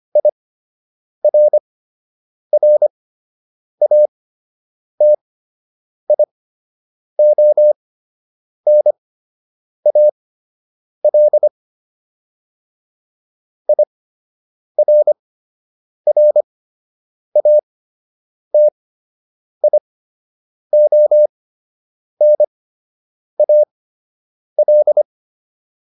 Morse code, heard as a keyed tone, about 25 wpm, spaced as for 8 wpm, 600 Hz. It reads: IRRATIONAL IRRATIONAL